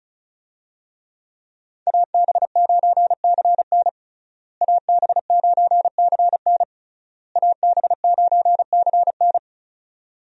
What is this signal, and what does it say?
Morse code, heard as a keyed tone, about 36 wpm, 700 Hz. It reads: AB9CN AB9CN AB9CN
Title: USA Call Signs - 5 Characters - Repeated 3x - 35wpm